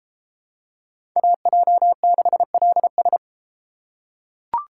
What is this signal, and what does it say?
Morse code, heard as a keyed tone, about 32 wpm, 700 Hz. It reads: AJ6LS